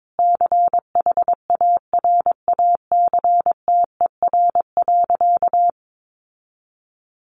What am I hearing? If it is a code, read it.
Morse code, heard as a keyed tone, about 22 wpm, 700 Hz. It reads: CHARACTER.
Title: Sentences from Top 1200 Words 22wpm